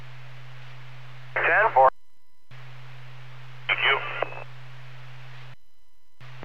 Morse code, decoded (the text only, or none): none